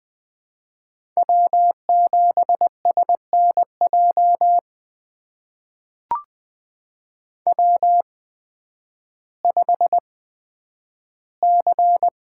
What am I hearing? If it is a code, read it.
Morse code, heard as a keyed tone, about 20 wpm, spaced as for 11 wpm, 700 Hz. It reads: W7SNJ W5C
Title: USA Call Signs - 5 Characters - 8x Character Spacing - 20wpm